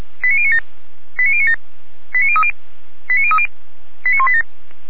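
Five-tone selective call (ZVEI), heard as quick runs of five tones, tones of about 70 ms. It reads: 89097 89097 89030 89030 89187
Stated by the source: ZVEI five-tone selective call 89229 received by SDR.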